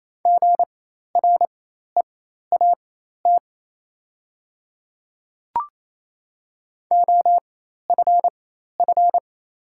Morse code, heard as a keyed tone, about 28 wpm, spaced as for 13 wpm, 700 Hz. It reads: GREAT OFF
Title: 4x Character Spacing - Top 300 Words - 28wpm